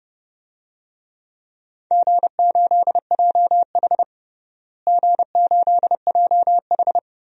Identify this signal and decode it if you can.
Morse code, heard as a keyed tone, about 32 wpm, 700 Hz. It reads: G8JH G8JH
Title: DXCC-100 Call Signs - 4 Characters - Repeated 3x - 30wpm